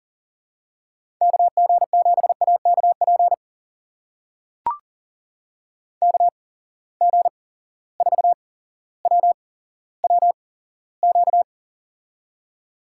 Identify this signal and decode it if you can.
Morse code, heard as a keyed tone, about 40 wpm, spaced as for 17 wpm, 700 Hz. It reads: KG7AKP KG4WWQ
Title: USA Call Signs - 6 Characters - 8x Character Spacing - 40wpm